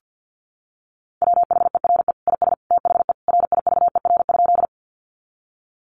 Morse code, heard as a keyed tone, about 50 wpm, 700 Hz. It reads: WHERE IS THE RIVER?
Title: Sentences from Top 300 Words 50wpm